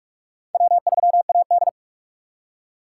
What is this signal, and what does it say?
Morse code, heard as a keyed tone, about 44 wpm, 700 Hz. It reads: W3AD
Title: USA Call Signs - 4 Characters - 6x Character Spacing - 45wpm